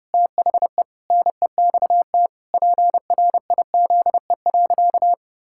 Morse code, heard as a keyed tone, about 30 wpm, 700 Hz. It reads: THE NEXT PRIZE.
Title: Pangram Challenge v2 - 30wpm